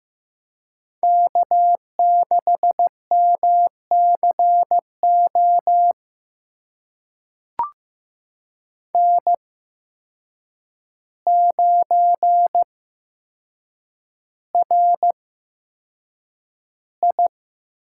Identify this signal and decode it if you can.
Morse code, heard as a keyed tone, about 15 wpm, spaced as for 7 wpm, 700 Hz. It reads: K6MCO N9RI
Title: USA Call Signs - 5 Characters - 8x Character Spacing - 15wpm